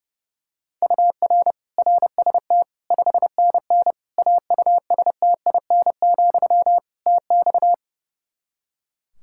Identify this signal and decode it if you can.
Morse code, heard as a keyed tone, about 30 wpm, 700 Hz. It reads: UR RST 5NN AUSTIN, TX
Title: Practice QSO - Exchange 1 - Beginning - 30wpm